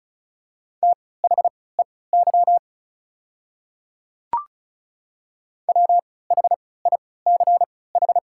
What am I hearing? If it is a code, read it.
Morse code, heard as a keyed tone, about 34 wpm, spaced as for 20 wpm, 690 Hz. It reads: THEY WHICH
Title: 3x Character Spacing - Top 100 Words - 35wpm